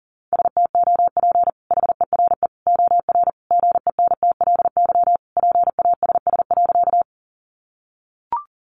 Morse code, heard as a keyed tone, about 40 wpm, 700 Hz. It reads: STOP HERE OR GENTLY PASS.
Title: Sentences from Top 1500 Words 40wpm